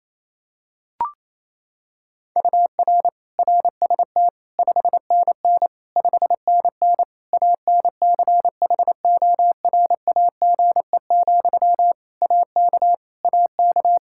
Morse code, heard as a keyed tone, about 28 wpm, 700 Hz. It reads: UR RST 5NN 5NN ANCHORAGE, AK AK
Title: Practice QSO - Exchange 1 - Beginning - 28wpm